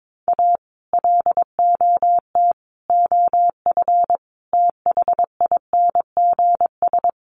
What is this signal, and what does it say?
Morse code, heard as a keyed tone, about 22 wpm, 700 Hz. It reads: A LOT OF THINGS
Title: Sets of 5 Words 22wpm